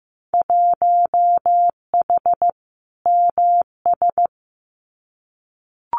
Morse code, heard as a keyed tone, about 15 wpm, 700 Hz. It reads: 1H MS
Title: Winter Field Day Exchanges 15wpm